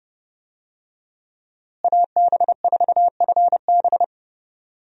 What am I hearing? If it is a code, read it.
Morse code, heard as a keyed tone, about 32 wpm, 700 Hz. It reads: AB4FB
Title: USA Call Signs - 5 Characters - Repeated 3x - 30wpm